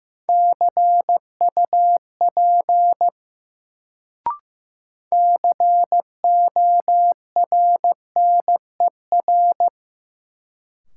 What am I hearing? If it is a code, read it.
Morse code, heard as a keyed tone, about 15 wpm, 700 Hz. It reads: CUP CORNER
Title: Sentences from Top 1600 Words - New Word Drill Alt - 15wpm